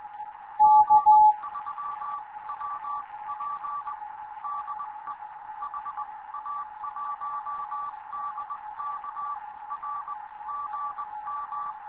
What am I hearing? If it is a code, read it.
Morse code, heard as a keyed tone, about 19 wpm, spaced as for 17 wpm, 1160 Hz. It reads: G3WP DE HA1DK R GM
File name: Morze-HA2MN / hangp / 2 11 GYAKORLAT